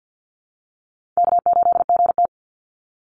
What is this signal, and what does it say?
Morse code, heard as a keyed tone, about 50 wpm, 700 Hz. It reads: K8GT
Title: Call Signs - 50wpm